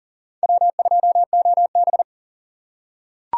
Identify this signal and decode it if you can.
Morse code, heard as a keyed tone, about 40 wpm, 700 Hz. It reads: W2OB
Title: USA Call Signs - 4 Characters - 4x Character Spacing - 40wpm